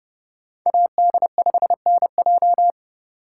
Morse code, heard as a keyed tone, about 28 wpm, 700 Hz. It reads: AD5NJ